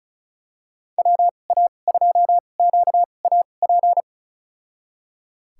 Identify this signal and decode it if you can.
Morse code, heard as a keyed tone, about 36 wpm, spaced as for 25 wpm, 700 Hz. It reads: WA2QAP
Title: USA Call Signs - 6 Characters - 2x Character Spacing - 35wpm